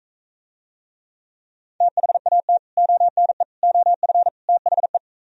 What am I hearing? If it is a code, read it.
Morse code, heard as a keyed tone, about 42 wpm, 700 Hz. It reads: THAT ONE OF THE